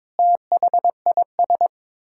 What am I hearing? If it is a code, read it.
Morse code, heard as a keyed tone, about 21 wpm, 700 Hz. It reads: THIS
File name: practice / 22wpm / Sets of 8 Words - 22wpm